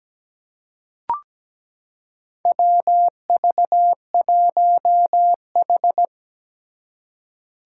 Morse code, heard as a keyed tone, about 17 wpm, 700 Hz. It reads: WV1H
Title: Four Challenging Letter-Number Combinations 17wpm